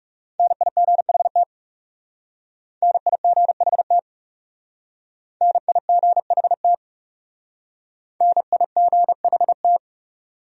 Morse code, heard as a keyed tone, about 39 wpm, 700 Hz. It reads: NIGHT NIGHT NIGHT NIGHT